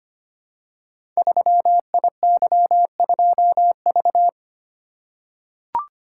Morse code, heard as a keyed tone, about 25 wpm, 700 Hz. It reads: 3IY2V
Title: Code Groups - 5 Letter:Numbers - 25wpm